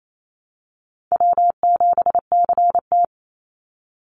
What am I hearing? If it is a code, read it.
Morse code, heard as a keyed tone, about 28 wpm, 700 Hz. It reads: W7CT
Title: Easy Call Signs Repeated 2x - 28wpm